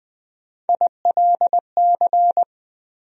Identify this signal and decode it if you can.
Morse code, heard as a keyed tone, about 20 wpm, 700 Hz. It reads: ILC
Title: USA Call Sign Postfixes - 3 Letters 20wpm